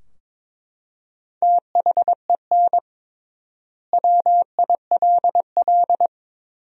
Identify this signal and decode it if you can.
Morse code, heard as a keyed tone, about 22 wpm, 700 Hz. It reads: THEN WILL
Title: 3x Word-Spacing - Sentences from Top 200 Words - 22wpm